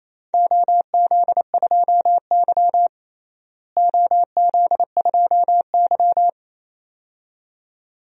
Morse code, heard as a keyed tone, about 28 wpm, 700 Hz. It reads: OZ2Y OZ2Y
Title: DXCC-100 Call Signs - 4 Characters - Repeated 2x - 28wpm